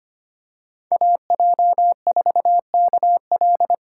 Morse code, heard as a keyed tone, about 25 wpm, 700 Hz. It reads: AJ4KL
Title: USA Call Signs - 5 Characters - 2x Character Spacing - 25wpm